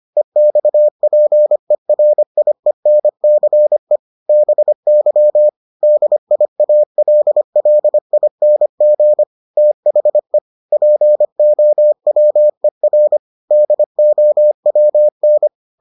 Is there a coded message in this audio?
Yes: Morse code from 0.2 to 15.5 s, heard as a keyed tone, about 25 wpm, 600 Hz. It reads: EXPERIENCE BY DIALLING THE POWER DOWN